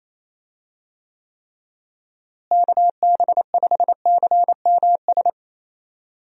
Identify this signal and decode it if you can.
Morse code, heard as a keyed tone, about 28 wpm, 700 Hz. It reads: KB5CMS
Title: USA Call Signs - 6 Characters - Repeated 3x - 28wpm